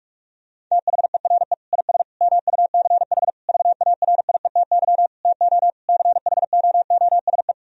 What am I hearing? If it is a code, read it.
Morse code, heard as a keyed tone, about 41 wpm, 700 Hz. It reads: THERE IS MUCH VARIETY TO CHOOSE
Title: Sentences from Top 1000 Words - Review Part 4 - 45wpm